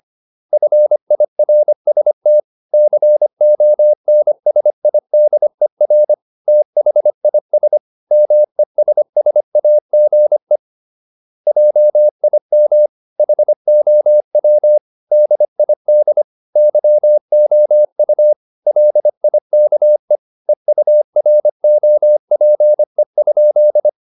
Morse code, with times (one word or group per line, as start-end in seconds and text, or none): 0.5-2.4 FIRST
2.7-6.1 CONSIDER
6.5-7.8 THIS
8.1-10.6 MESSAGE
11.5-12.9 JIM
13.2-14.8 HOW
15.1-16.2 DID
16.6-18.3 YOU
18.7-20.2 LIKE
20.5-23.9 EUROPE?